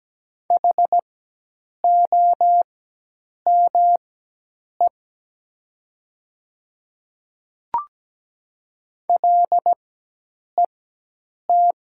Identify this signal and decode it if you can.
Morse code, heard as a keyed tone, about 17 wpm, spaced as for 8 wpm, 700 Hz. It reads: HOME LET
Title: 4x Character Spacing - Top 300 Words - 17wpm